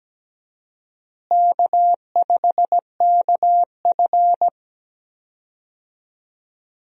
Morse code, heard as a keyed tone, about 17 wpm, 700 Hz. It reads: K5KF
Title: USA Call Signs - 4 Characters - Repeated 3x - 17wpm